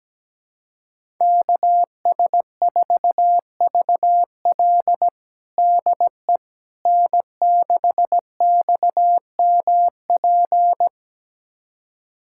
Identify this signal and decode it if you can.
Morse code, heard as a keyed tone, about 17 wpm, 700 Hz. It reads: KS4VL DE N6XMP
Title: USA Callsign DE Callsign - 5 Characters - 17wpm